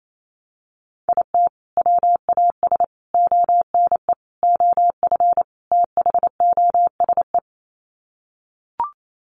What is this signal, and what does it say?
Morse code, heard as a keyed tone, about 28 wpm, 700 Hz. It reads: IT WAS ONE OF THOSE
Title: Sets of 5 Words Encore 28wpm